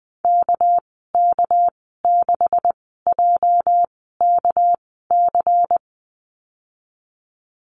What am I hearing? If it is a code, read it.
Morse code, heard as a keyed tone, about 20 wpm, spaced as for 14 wpm, 700 Hz. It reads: KK6JKC
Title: 245 Lesson 39g - Letter z - Call Signs 20wpm